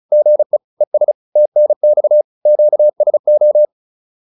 Morse code, heard as a keyed tone, about 34 wpm, 600 Hz. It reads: GE ES TNX QSO
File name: QSO207 35